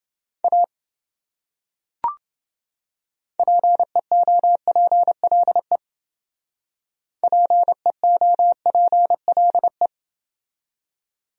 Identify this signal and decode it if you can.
Morse code, heard as a keyed tone, about 29 wpm, 700 Hz. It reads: A PEOPLE PEOPLE